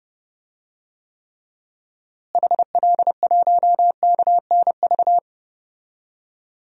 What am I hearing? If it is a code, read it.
Morse code, heard as a keyed tone, about 32 wpm, 700 Hz. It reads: HL1KNV
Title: DXCC-100 Call Signs - 6 Characters - Repeated 3x - 30wpm